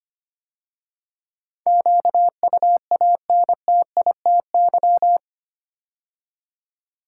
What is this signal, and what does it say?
Morse code, heard as a keyed tone, about 25 wpm, 700 Hz. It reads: QUANTITY
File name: 8 Letter Words - First 3 Letters Given - 25wpm